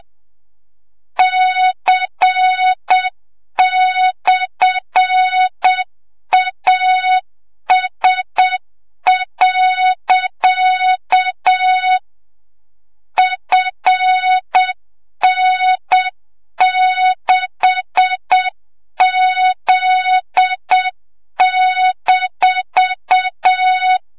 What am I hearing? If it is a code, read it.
Morse code, heard as a keyed tone, about 7 wpm, 3700 Hz, with noise. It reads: C/AS. FN6Z-
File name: Morze-HA2MN / hang / 25 GYAKORLAT